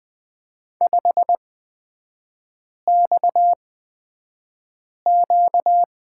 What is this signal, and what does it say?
Morse code, heard as a keyed tone, about 20 wpm, spaced as for 5 wpm, 700 Hz. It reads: 5XQ
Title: 255 Lesson 41e - Period - Three Characters 20wpm